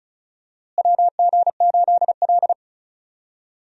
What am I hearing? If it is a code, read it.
Morse code, heard as a keyed tone, about 36 wpm, 700 Hz. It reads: WG8L
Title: USA Call Signs - 4 Characters - Repeated 3x - 35wpm